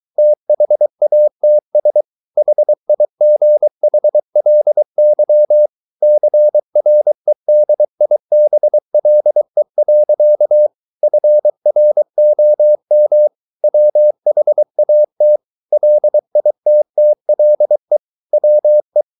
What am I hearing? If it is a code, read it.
Morse code, heard as a keyed tone, about 23 wpm, 600 Hz. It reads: THATS HIGHLY CREDIBLE. FROM WHAT LITTLE WE